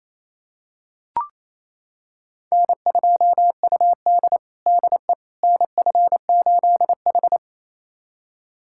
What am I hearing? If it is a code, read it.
Morse code, heard as a keyed tone, about 27 wpm, 700 Hz. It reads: N2UD DE NF8H
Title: USA Callsign DE Callsign - 4 Characters - 28wpm